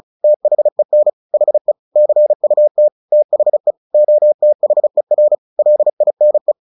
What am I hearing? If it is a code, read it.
Morse code, heard as a keyed tone, about 34 wpm, 600 Hz. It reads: THEN HE CUT THE OTHER LINE
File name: the old man and the sea / the old man and the sea 1 35wpm